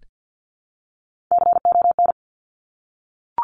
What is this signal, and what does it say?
Morse code, heard as a keyed tone, about 50 wpm, 700 Hz. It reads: CON
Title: Top Three Letter Combinations in Words 50wpm